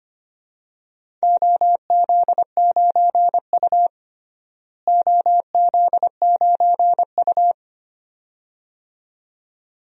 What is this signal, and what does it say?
Morse code, heard as a keyed tone, about 25 wpm, 700 Hz. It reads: OZ9U OZ9U
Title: DXCC-100 Call Signs - 4 Characters - Repeated 2x - 25wpm